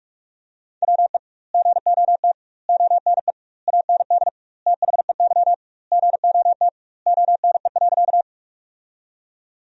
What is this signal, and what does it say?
Morse code, heard as a keyed tone, about 46 wpm, 700 Hz. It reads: WE GOT ONE AND THEY GOT ONE.
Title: 2x Word-Spacing - Sentences from Top 300 Words - 45wpm